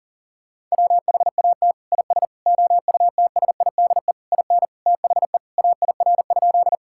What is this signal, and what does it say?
Morse code, heard as a keyed tone, about 37 wpm, 700 Hz. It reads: WHAT IS OUTSIDE IN THE AIR?
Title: Sentences from Top 1000 Words - Review Part 6 - 40wpm